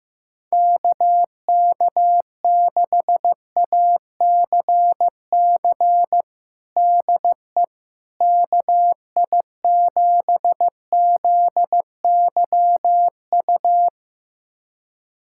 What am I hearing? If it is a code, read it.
Morse code, heard as a keyed tone, about 15 wpm, 700 Hz. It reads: KK6ACC DE KI7ZYU
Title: USA Callsign DE Callsign - 6 Characters - 15wpm